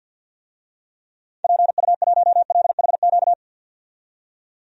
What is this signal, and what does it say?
Morse code, heard as a keyed tone, about 53 wpm, 700 Hz. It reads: PU1LHQ
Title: DXCC-100 Call Signs - 6 Characters - 50wpm